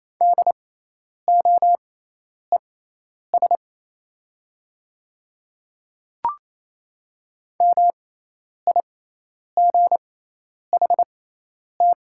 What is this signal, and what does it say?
Morse code, heard as a keyed tone, about 28 wpm, spaced as for 10 wpm, 700 Hz. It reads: DOES MIGHT
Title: 6x Character Spacing - Top 300 Words - 28wpm